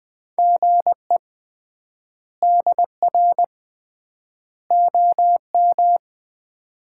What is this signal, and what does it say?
Morse code, heard as a keyed tone, about 20 wpm, 700 Hz. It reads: GE DR OM
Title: Common QSO Phrases - 3x Word-Spacing - 20wpm